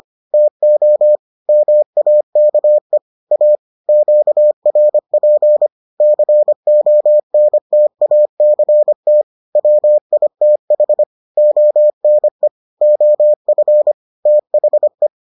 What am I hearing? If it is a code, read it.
Morse code, heard as a keyed tone, about 25 wpm, 600 Hz. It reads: TO MAKE A QRP CONTACT WITH ONE OF THE